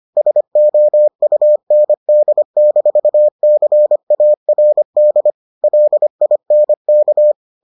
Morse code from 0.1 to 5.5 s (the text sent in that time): SOUND-CARD